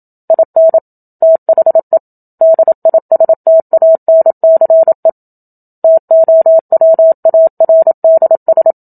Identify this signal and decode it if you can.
Morse code, heard as a keyed tone, about 27 wpm, 650 Hz. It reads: IN THE DISTANCE TOWARDS
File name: Book Two / 01-Chapter 1 - 27 WPM